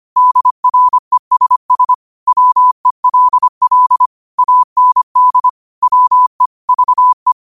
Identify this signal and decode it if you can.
Morse code, heard as a keyed tone, about 25 wpm, 1000 Hz. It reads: DRESS WELL AND WE VE